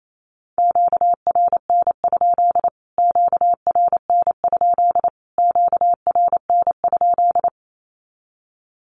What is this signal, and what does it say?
Morse code, heard as a keyed tone, about 28 wpm, 700 Hz. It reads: QRN? QRN? QRN?
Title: Q-Codes Repeated 3x 28wpm